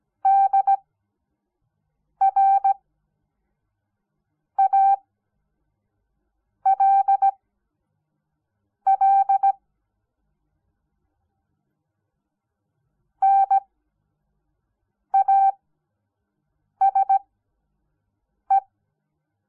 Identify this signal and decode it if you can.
Morse code, heard as a keyed tone, about 17 wpm, spaced as for 5 wpm, 780 Hz. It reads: DRALL NASE
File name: Onlinelehrgang / morsen / DJ4UF Morsen06